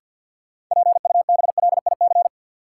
Morse code, heard as a keyed tone, about 52 wpm, 700 Hz. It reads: PUBLIC
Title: Sets of 2 Words - New Word Drill - 50wpm